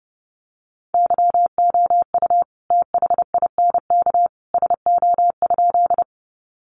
Morse code, heard as a keyed tone, about 30 wpm, 700 Hz. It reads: YOU THINK SO?
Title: Sentences from Top 200 Words 30wpm